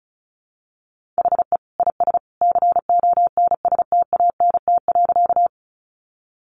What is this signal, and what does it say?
Morse code, heard as a keyed tone, about 34 wpm, 700 Hz. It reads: HE IS CONSTANT.
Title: Sentences from Top 1500 Words 35wpm